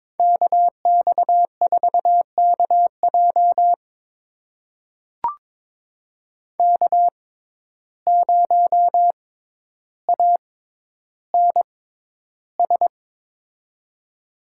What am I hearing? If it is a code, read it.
Morse code, heard as a keyed tone, about 22 wpm, spaced as for 11 wpm, 700 Hz. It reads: KX4KJ K0ANS
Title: USA Call Signs - 5 Characters - 6x Character Spacing - 22wpm